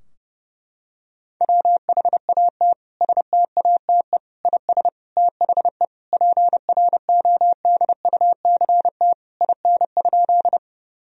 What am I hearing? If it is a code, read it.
Morse code, heard as a keyed tone, about 30 wpm, 700 Hz. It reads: WHAT STATE IS THE PRODUCT IN?
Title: Sentences from Top 1000 Words - Review Part 2 - 30wpm